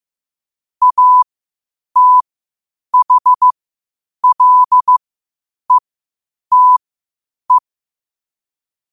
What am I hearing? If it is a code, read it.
Morse code, heard as a keyed tone, about 15 wpm, spaced as for 8 wpm, 1000 Hz. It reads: ATHLETE
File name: qotdfiles / 8wpm